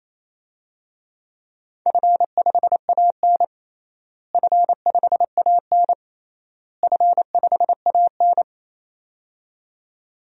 Morse code, heard as a keyed tone, about 28 wpm, 700 Hz. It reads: F5AN F5AN F5AN
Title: DXCC-100 Call Signs - 4 Characters - Repeated 3x - 28wpm